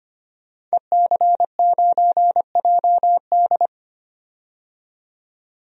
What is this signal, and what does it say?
Morse code, heard as a keyed tone, about 25 wpm, 700 Hz. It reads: EC9JD
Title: DXCC-100 Call Signs - 5 Characters - 25wpm